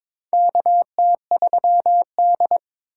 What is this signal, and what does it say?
Morse code, heard as a keyed tone, about 22 wpm, 700 Hz. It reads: KT3D
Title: USA Call Signs - 4 Characters - 4x Character Spacing - 22wpm